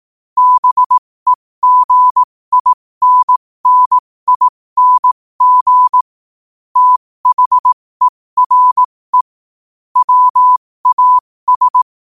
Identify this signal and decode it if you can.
Morse code, heard as a keyed tone, about 18 wpm, spaced as for 15 wpm, 1000 Hz. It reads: BEGINNING THERE WAS
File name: qotdfiles / 15wpm